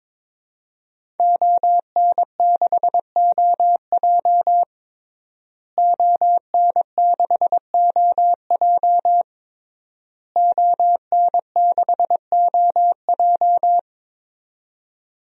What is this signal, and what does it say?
Morse code, heard as a keyed tone, about 22 wpm, 700 Hz. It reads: ON6OJ ON6OJ ON6OJ
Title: DXCC-100 Call Signs - 5 Characters - Repeated 3x - 22wpm